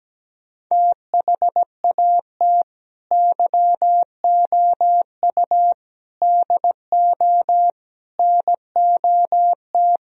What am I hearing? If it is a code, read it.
Morse code, heard as a keyed tone, about 17 wpm, 700 Hz. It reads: THAT YOU DO NOT
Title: Speed-Racing - Sets of 4 Words 17wpm